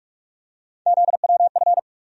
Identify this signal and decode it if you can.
Morse code, heard as a keyed tone, about 46 wpm, 700 Hz. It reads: ZWF
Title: USA Call Sign Postfixes - 3 Letters 45wpm